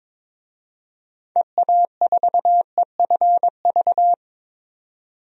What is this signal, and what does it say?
Morse code, heard as a keyed tone, about 22 wpm, 700 Hz. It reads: EA4EFV
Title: DXCC-100 Call Signs - 6 Characters - 22wpm